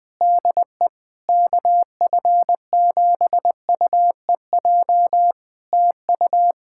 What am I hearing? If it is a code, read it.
Morse code, heard as a keyed tone, about 21 wpm, 700 Hz. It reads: DE KF7UEJ TU
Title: Practice QSO - Exchange 4 - Ending - 20wpm